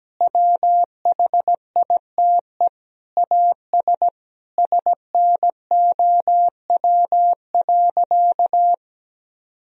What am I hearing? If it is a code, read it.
Morse code, heard as a keyed tone, about 17 wpm, 700 Hz. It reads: WHITE AS SNOW.
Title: Sentences from Top 1000 Words - Review Part 3 - 17wpm